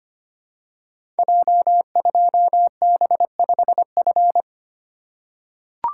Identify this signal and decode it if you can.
Morse code, heard as a keyed tone, about 25 wpm, 700 Hz. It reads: J2B5F
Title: Code Groups - 5 Letter:Numbers - 25wpm